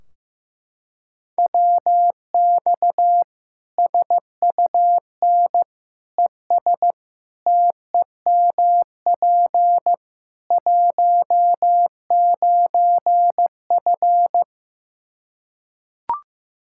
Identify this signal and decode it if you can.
Morse code, heard as a keyed tone, about 15 wpm, 700 Hz. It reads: WX SUN ES TEMP 19F
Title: Practice QSO - Exchange 2 - Info - 15wpm